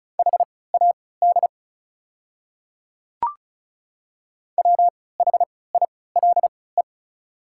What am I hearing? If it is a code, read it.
Morse code, heard as a keyed tone, about 36 wpm, spaced as for 20 wpm, 690 Hz. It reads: HAD WHILE
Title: 3x Character Spacing - Top 300 Words - 35wpm